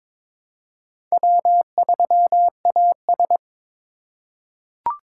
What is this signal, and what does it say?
Morse code, heard as a keyed tone, about 23 wpm, 700 Hz. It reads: W3AS
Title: USA Call Signs - 4 Characters - 2x Character Spacing - 22wpm